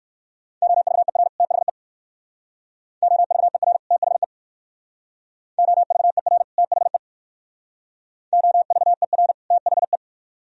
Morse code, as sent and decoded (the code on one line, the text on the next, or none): --- ...- . .-. / - .... / --- ...- . .-. / - .... / --- ...- . .-. / - .... / --- ...- . .-. / - .... .
OVER THE OVER THE OVER THE OVER THE